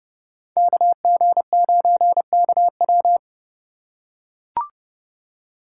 Morse code, heard as a keyed tone, about 32 wpm, 700 Hz. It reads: KG9KW
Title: USA Call Signs - 5 Characters - 3x Character Spacing - 30wpm